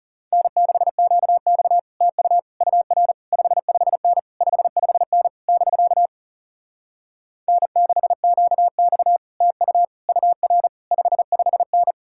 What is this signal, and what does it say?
Morse code, heard as a keyed tone, about 40 wpm, 700 Hz. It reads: N6QX TU UR 55N 55N <BK> N6QX TU UR 55N